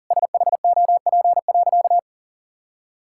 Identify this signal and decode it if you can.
Morse code, heard as a keyed tone, about 40 wpm, 700 Hz. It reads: SHOP.
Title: Sentences from Top 1600 Words 40wpm